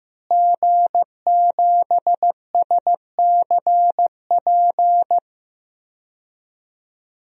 Morse code, heard as a keyed tone, about 16 wpm, 700 Hz. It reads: G7SCP
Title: DXCC-100 Call Signs - 5 Characters - Repeated 2x - 15wpm